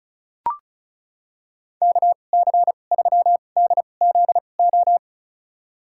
Morse code, heard as a keyed tone, about 36 wpm, spaced as for 25 wpm, 700 Hz. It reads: KC3DZO
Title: USA Call Signs - 6 Characters - 2x Character Spacing - 35wpm